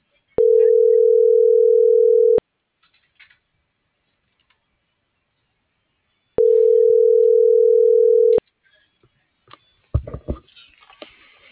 Ringback tone (North America), 440 + 480 Hz.